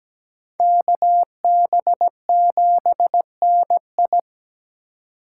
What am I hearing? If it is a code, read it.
Morse code, heard as a keyed tone, about 20 wpm, 700 Hz. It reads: KB7NI
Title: Speed Racing - USA 2x2 Call Signs 17wpm